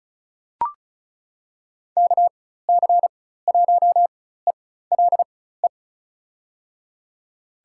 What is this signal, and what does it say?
Morse code, heard as a keyed tone, about 36 wpm, spaced as for 16 wpm, 700 Hz. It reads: KC1ELE
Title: USA Call Signs - 6 Characters - 4x Character Spacing - 35wpm